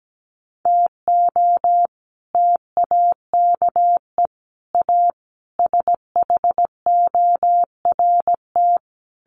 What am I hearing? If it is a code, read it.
Morse code, heard as a keyed tone, about 17 wpm, 700 Hz. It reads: TO TAKE A SHORT
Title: Sets of 5 Words Encore2 17wpm